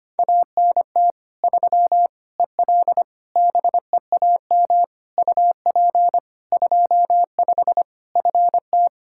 Morse code, heard as a keyed tone, about 25 wpm, 700 Hz. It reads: ANT 3 EL BEAM UP 25 FT
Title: Practice QSO - Exchange 2 - Info - 25wpm